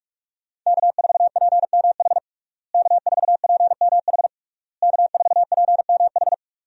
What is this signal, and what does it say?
Morse code, heard as a keyed tone, about 41 wpm, 700 Hz. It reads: K4PMH K4PMH K4PMH
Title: USA Call Signs - 5 Characters - Repeated 3x - 45wpm